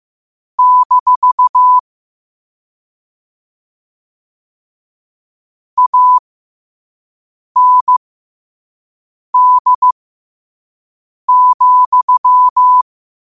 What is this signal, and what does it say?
Morse code, heard as a keyed tone, about 15 wpm, spaced as for 5 wpm, 1000 Hz. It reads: - AND,
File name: qotdfiles / 5wpm